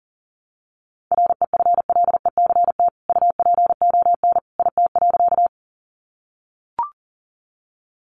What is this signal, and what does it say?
Morse code, heard as a keyed tone, about 40 wpm, 700 Hz. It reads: REFLECT UPON IT.
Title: Sentences from Top 1500 Words 40wpm